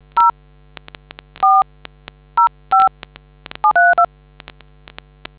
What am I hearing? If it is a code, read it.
DTMF, heard as telephone touch tones, tones of about 150 ms. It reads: 0 405732